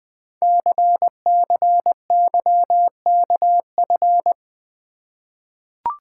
Morse code, heard as a keyed tone, about 20 wpm, 700 Hz. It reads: CCYKF
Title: Code Groups - 5 Letters - 20wpm